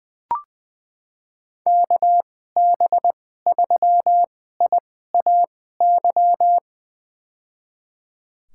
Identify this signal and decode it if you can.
Morse code, heard as a keyed tone, about 20 wpm, spaced as for 15 wpm, 700 Hz. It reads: KB3IAY
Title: USA Call Signs - 6 Characters - 2x Character Spacing - 20wpm